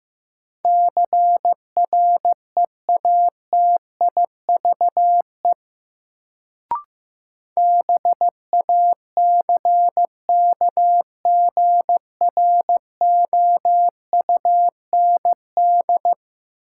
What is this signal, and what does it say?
Morse code, heard as a keyed tone, about 15 wpm, 700 Hz. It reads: CREATIVE BACKGROUND
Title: Sentences from Top 1300 Words - New Word Drill Alt - 15wpm